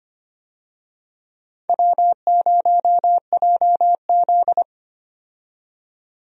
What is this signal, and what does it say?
Morse code, heard as a keyed tone, about 25 wpm, 700 Hz. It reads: W0JZ
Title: USA Call Signs - 4 Characters - 25wpm